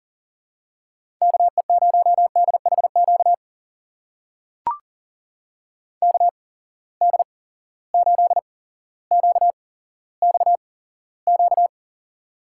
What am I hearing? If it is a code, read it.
Morse code, heard as a keyed tone, about 40 wpm, spaced as for 17 wpm, 700 Hz. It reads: KE0DHQ KD8QXQ
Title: USA Call Signs - 6 Characters - 8x Character Spacing - 40wpm